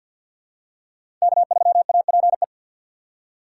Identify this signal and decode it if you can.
Morse code, heard as a keyed tone, about 47 wpm, 700 Hz. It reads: K3APE
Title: USA Call Signs - 4x Character Spacing - 50wpm